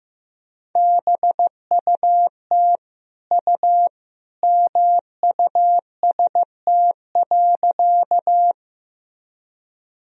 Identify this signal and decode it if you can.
Morse code, heard as a keyed tone, about 15 wpm, 700 Hz. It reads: BUT U MUST.
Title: Sentences from Top 200 Words and Abbreviations 15wpm